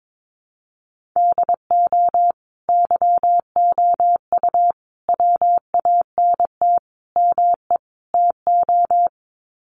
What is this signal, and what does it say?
Morse code, heard as a keyed tone, about 22 wpm, 700 Hz. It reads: DO YOU WANT ME TO
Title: Sets of 5 Words 22wpm